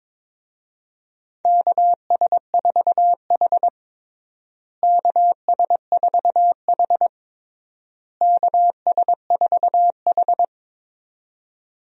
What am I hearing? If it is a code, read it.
Morse code, heard as a keyed tone, about 22 wpm, 700 Hz. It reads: KS4H KS4H KS4H
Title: USA Call Signs - 4 Characters - Repeated 3x - 22wpm